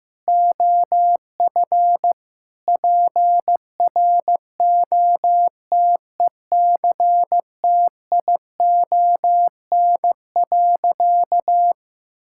Morse code, heard as a keyed tone, about 15 wpm, 700 Hz. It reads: OF PROTECTION.
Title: Sentences from Top 1600 Words 15wpm